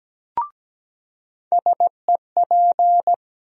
Morse code, heard as a keyed tone, about 17 wpm, 700 Hz. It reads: SEP
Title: USA Call Sign Postfixes - 3 Letters 17wpm